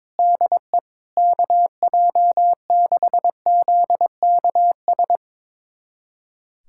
Morse code, heard as a keyed tone, about 22 wpm, 700 Hz. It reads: DE KJ6ZKS
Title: USA Callsign DE Callsign - 6 Characters - 22wpm